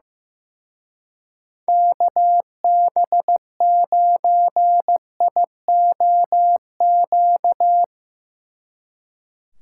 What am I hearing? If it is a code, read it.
Morse code, heard as a keyed tone, about 16 wpm, 700 Hz. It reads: KB9IOQ
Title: Speed Racing - USA 2x3 Call Signs 15wpm